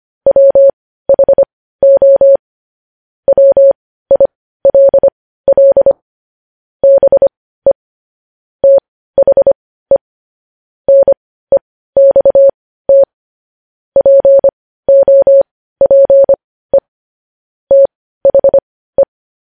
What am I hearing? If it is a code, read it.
Morse code, heard as a keyed tone, about 25 wpm, spaced as for 15 wpm, 550 Hz. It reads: WHO WILL BE THE NEXT POPE THE
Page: news headlines in morse code at 15 wpm